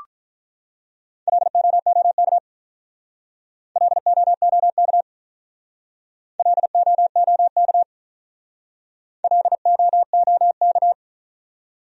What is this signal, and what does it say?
Morse code, heard as a keyed tone, about 46 wpm, 700 Hz. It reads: LOOK LOOK LOOK LOOK